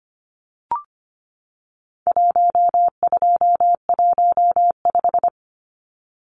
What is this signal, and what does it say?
Morse code, heard as a keyed tone, about 25 wpm, 700 Hz. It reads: 1215
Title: Numbers - 4-Digit 25wpm